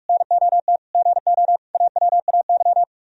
Morse code, heard as a keyed tone, about 49 wpm, 700 Hz. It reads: NOT GO AWAY